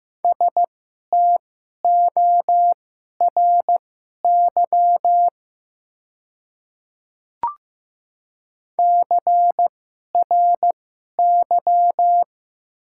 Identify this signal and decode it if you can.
Morse code, heard as a keyed tone, about 15 wpm, spaced as for 11 wpm, 690 Hz. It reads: STORY CRY
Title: 2x Character Spacing - Top 500 Words - 15wpm